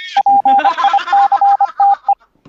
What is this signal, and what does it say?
Morse code, heard as a keyed tone, about 25 wpm, 800 Hz, with speech. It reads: PICTE